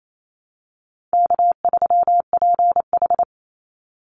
Morse code, heard as a keyed tone, about 28 wpm, 700 Hz. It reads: K3PH
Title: Easy Call Signs Repeated 3x - 28wpm